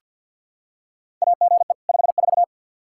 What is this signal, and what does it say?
Morse code, heard as a keyed tone, about 49 wpm, 700 Hz. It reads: AGE 54
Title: Practice QSO - Exchange 3 - More Info - 50wpm